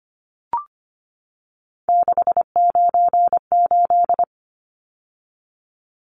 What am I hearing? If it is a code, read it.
Morse code, heard as a keyed tone, about 25 wpm, 700 Hz. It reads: 698
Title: Numbers - 3-Digit 25wpm